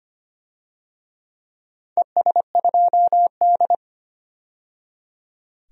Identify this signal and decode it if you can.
Morse code, heard as a keyed tone, about 25 wpm, 690 Hz. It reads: ES2D
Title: DXCC-100 Call Signs - 4 Characters - 25wpm